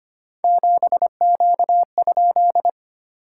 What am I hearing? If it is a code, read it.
Morse code, heard as a keyed tone, about 25 wpm, 700 Hz. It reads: 7Q?
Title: SOTA QSO Practice 25wpm